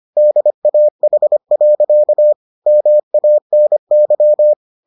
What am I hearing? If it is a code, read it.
Morse code, heard as a keyed tone, about 25 wpm, 600 Hz. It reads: DAH. MANY